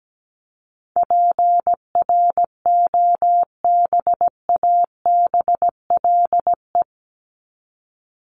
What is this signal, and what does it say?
Morse code, heard as a keyed tone, about 17 wpm, 700 Hz. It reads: PROBABLE
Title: LY Suffix - Top 100 Words with Base Words 17wpm